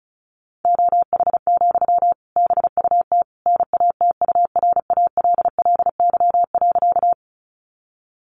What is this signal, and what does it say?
Morse code, heard as a keyed tone, about 34 wpm, 700 Hz. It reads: OH, BUT NATURALLY.
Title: Sentences from Top 1300 Words 35wpm